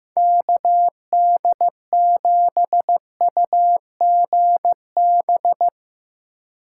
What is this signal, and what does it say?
Morse code, heard as a keyed tone, about 16 wpm, 700 Hz. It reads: KD7UGB
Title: USA Call Signs - 6 Characters - 3x Character Spacing - 15wpm